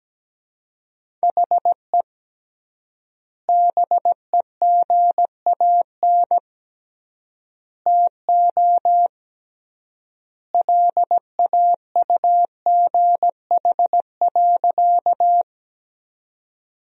Morse code, heard as a keyed tone, about 17 wpm, 700 Hz. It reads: HE BEGAN TO LAUGH.
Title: 3x Word-Spacing - Sentences from Top 500 Words - 17wpm